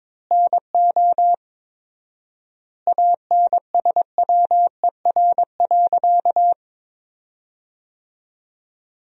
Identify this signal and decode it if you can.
Morse code, heard as a keyed tone, about 22 wpm, 700 Hz. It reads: NO ANSWER.
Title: 4x Word-Spacing - Sentences from Top 300 Words - 22wpm